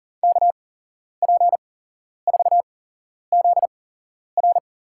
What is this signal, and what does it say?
Morse code, heard as a keyed tone, about 40 wpm, spaced as for 11 wpm, 700 Hz. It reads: KP4ZR
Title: USA Call Signs - 5 Characters - 8x Character Spacing - 40wpm